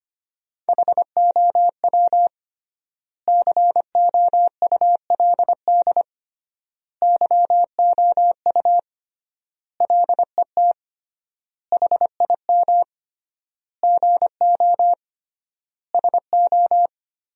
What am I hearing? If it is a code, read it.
Morse code, heard as a keyed tone, about 25 wpm, 700 Hz. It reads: HOW COULD YOU LET HIM GO SO
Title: 3x Word-Spacing - Sentences from Top 300 Words - 25wpm